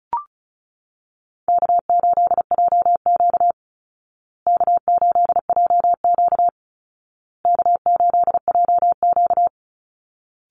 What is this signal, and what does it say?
Morse code, heard as a keyed tone, about 34 wpm, 700 Hz. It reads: K8JQ K8JQ K8JQ
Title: Easy Call Signs Repeated 3x - 35wpm